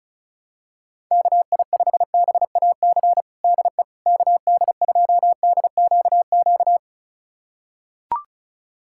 Morse code, heard as a keyed tone, about 36 wpm, 700 Hz. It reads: KI5BAC DE KD2DQQ
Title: USA Callsign DE Callsign - 6 Characters - 35wpm